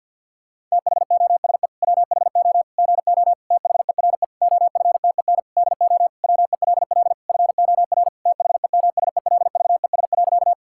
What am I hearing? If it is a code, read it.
Morse code, heard as a keyed tone, about 50 wpm, 700 Hz. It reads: THOSE WHO GO THERE OFTEN DO WELL FOR THEMSELVES.
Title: Sentences from Top 1000 Words - Review Part 5 - 50wpm